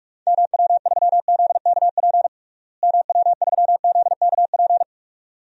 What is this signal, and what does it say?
Morse code, heard as a keyed tone, about 43 wpm, 700 Hz. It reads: MW3ZKP MW3ZKP
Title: DXCC-100 Call Signs - 6 Characters - Repeated 2x - 45wpm